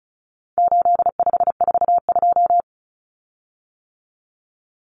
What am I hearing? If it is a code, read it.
Morse code, heard as a keyed tone, about 34 wpm, 700 Hz. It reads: 8542
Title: Common Cut Numbers - 4-Digit 35wpm